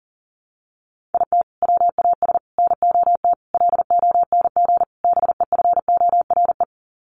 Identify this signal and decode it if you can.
Morse code, heard as a keyed tone, about 40 wpm, 700 Hz. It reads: IT WAS NOT LONG BEFORE
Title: Sets of 5 Words Encore2 40wpm